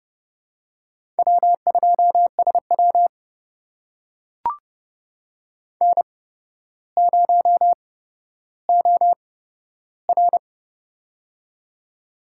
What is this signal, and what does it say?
Morse code, heard as a keyed tone, about 30 wpm, spaced as for 13 wpm, 700 Hz. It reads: W2SW N0OR
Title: USA Call Signs - 4 Characters - 8x Character Spacing - 30wpm